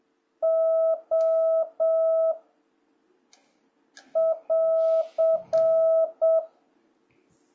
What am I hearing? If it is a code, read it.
Morse code, heard as a keyed tone, about 7 wpm, 660 Hz. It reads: O <AR>
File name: jh8cbh / 2026cw